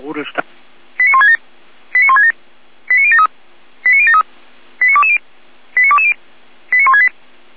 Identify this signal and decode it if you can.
Five-tone selective call (ZVEI), heard as quick runs of five tones, tones of about 70 ms, with speech. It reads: 89167 89167 89073 89073 89220 89220 89168